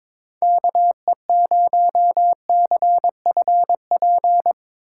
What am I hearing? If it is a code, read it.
Morse code, heard as a keyed tone, about 22 wpm, 700 Hz. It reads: KE0CFP
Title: USA Call Signs - 6 Characters - 2x Character Spacing - 22wpm